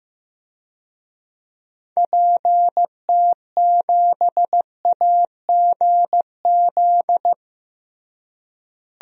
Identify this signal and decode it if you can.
Morse code, heard as a keyed tone, about 15 wpm, 700 Hz. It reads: PT7AGZ
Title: DXCC-100 Call Signs - 6 Characters - Repeated 3x - 15wpm